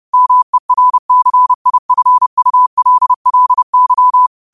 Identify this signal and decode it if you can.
Morse code, heard as a keyed tone, about 30 wpm, 1000 Hz. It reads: MERCIFULLY
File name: qotdfiles / 30wpm